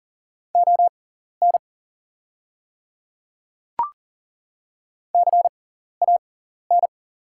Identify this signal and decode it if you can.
Morse code, heard as a keyed tone, about 39 wpm, spaced as for 14 wpm, 700 Hz. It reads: ON CAN
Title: 6x Character Spacing - Top 100 Words - 40wpm